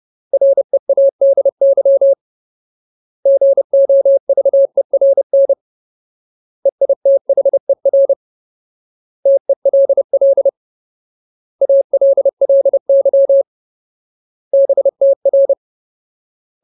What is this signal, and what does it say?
Morse code, heard as a keyed tone, about 30 wpm, 550 Hz. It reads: READY GOVERN EITHER TELL ALLY BTR
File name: Ultimate Word List / 30WPM Ultimate